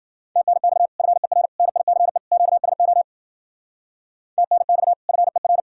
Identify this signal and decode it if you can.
Morse code, heard as a keyed tone, about 47 wpm, 700 Hz. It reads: TNX FER NICE QSO TNX FER
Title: Speed-Racing Common QSO Phrases Encore - 40wpm